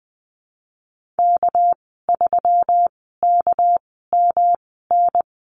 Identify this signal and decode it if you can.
Morse code, heard as a keyed tone, about 20 wpm, spaced as for 15 wpm, 700 Hz. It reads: K3KMN
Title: 200 Lesson 32g - Letter k - Call Signs 20wpm